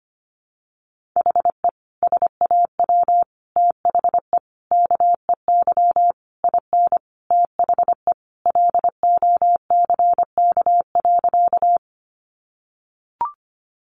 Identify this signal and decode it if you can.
Morse code, heard as a keyed tone, about 25 wpm, 700 Hz. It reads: HE SAW THE KEY IN THE LOCK.
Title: Sentences from Top 1300 Words 25wpm